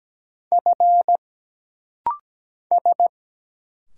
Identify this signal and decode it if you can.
Morse code, heard as a keyed tone, about 17 wpm, 700 Hz. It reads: F S
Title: Single Letter-Number Alt - Rapid-Fire 17wpm